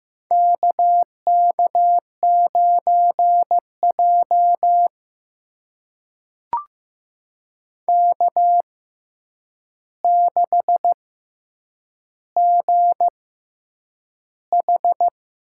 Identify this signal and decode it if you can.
Morse code, heard as a keyed tone, about 15 wpm, spaced as for 8 wpm, 700 Hz. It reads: KK9J K6GH
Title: USA Call Signs - 4 Characters - 6x Character Spacing - 15wpm